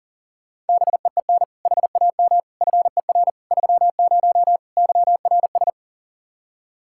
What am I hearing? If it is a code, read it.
Morse code, heard as a keyed tone, about 49 wpm, 700 Hz. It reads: BEEN HAM FER 30 YRS
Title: Speed-Racing Common QSO Phrases Encore - 40wpm